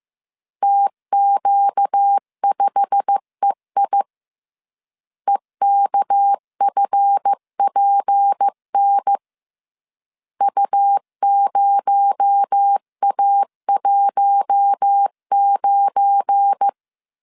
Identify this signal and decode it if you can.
Morse code, heard as a keyed tone, about 15 wpm, 800 Hz. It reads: TQ5EI EKFPN U0A19
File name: CW / DET / 1 9 1500C S A 14wpm 3222